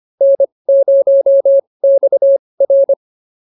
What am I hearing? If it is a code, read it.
Morse code, heard as a keyed tone, about 25 wpm, spaced as for 20 wpm, 550 Hz. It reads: N0XR